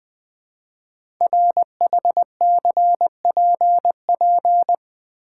Morse code, heard as a keyed tone, about 20 wpm, 700 Hz. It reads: RHCPP
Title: Code Groups - 5 Letters - 20wpm